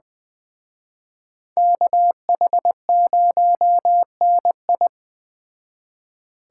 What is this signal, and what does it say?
Morse code, heard as a keyed tone, about 20 wpm, 700 Hz. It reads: KH0NI
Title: Speed Racing - USA 2x2 Call Signs 17wpm